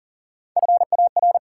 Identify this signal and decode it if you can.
Morse code, heard as a keyed tone, about 40 wpm, 700 Hz. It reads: FAR
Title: Sets of 3 Words - New Word Drill - 40wpm